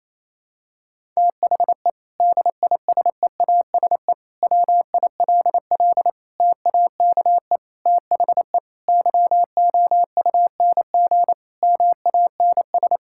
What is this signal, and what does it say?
Morse code, heard as a keyed tone, about 28 wpm, 700 Hz. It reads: THE DISEASE WILL TAKE THE YOUNG MANS
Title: Sentences from Top 1000 Words - Review Part 6 - 28wpm